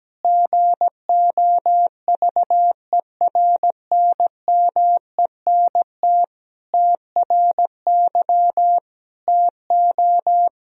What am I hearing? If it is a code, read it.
Morse code, heard as a keyed tone, about 17 wpm, 700 Hz. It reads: GOVERNMENT TRY TO
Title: Sentences from Top 1000 Words - Review Part 3 - 17wpm